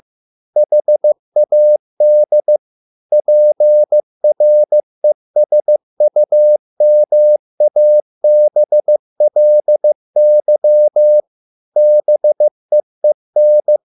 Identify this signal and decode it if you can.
Morse code, heard as a keyed tone, about 15 wpm, 600 Hz. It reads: HAD PRESUMABLY BEEN